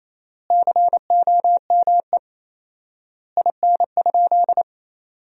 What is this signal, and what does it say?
Morse code, heard as a keyed tone, about 28 wpm, 700 Hz. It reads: COME IN?
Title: 4x Word-Spacing - Sentences from Top 100 Words - 28wpm